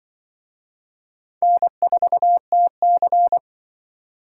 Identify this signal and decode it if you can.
Morse code, heard as a keyed tone, about 25 wpm, 700 Hz. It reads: N4TC
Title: Speed Racing - USA 1x2 Call Signs 20wpm